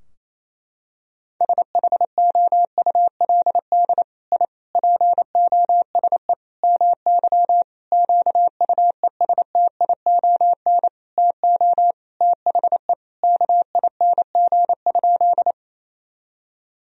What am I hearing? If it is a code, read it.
Morse code, heard as a keyed tone, about 28 wpm, 700 Hz. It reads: SHOULD I POSE MY QUESTION TO THE KING?
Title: Sentences from Top 1000 Words - Review Part 2 - 28wpm